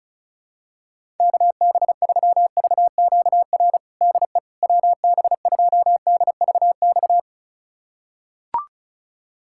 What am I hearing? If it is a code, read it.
Morse code, heard as a keyed tone, about 36 wpm, 700 Hz. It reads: KB3VQR DE WB2DVX
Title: USA Callsign DE Callsign - 6 Characters - 35wpm